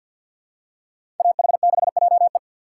Morse code, heard as a keyed tone, about 53 wpm, 700 Hz. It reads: AH6JE
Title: USA Call Signs - 5 Characters - 3x Character Spacing - 50wpm